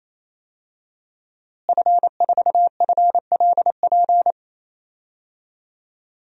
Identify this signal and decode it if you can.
Morse code, heard as a keyed tone, about 27 wpm, 700 Hz. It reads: F4FLP
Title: DXCC-100 Call Signs - 5 Characters - Repeated 3x - 28wpm